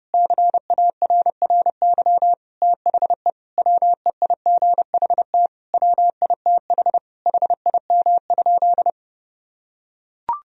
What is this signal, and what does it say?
Morse code, heard as a keyed tone, about 32 wpm, 700 Hz. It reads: CARRY THE WEIGHT WITH HIM?
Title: Sentences from Top 1000 Words - Review Part 3 - 30wpm